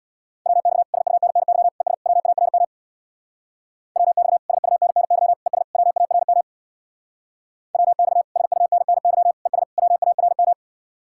Morse code, heard as a keyed tone, about 64 wpm, 700 Hz. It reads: WX SUNNY ES WARM WX SUNNY ES WARM WX SUNNY ES WARM